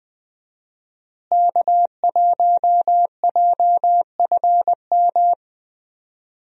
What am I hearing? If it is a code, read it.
Morse code, heard as a keyed tone, about 20 wpm, 700 Hz. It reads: K1JFM
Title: USA Call Signs - 5 Characters - Repeated 2x - 20wpm